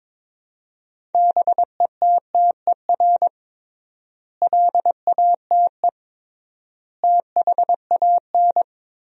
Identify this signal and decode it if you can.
Morse code, heard as a keyed tone, about 21 wpm, 700 Hz. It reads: BETTER LATE THAN
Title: 3x Word-Spacing - Sentences from Top 400 Words - 22wpm